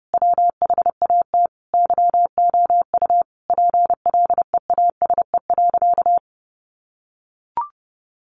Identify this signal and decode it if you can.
Morse code, heard as a keyed tone, about 30 wpm, 700 Hz. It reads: WHAT YOU PLEASE.
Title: Sentences from Top 1500 Words 30wpm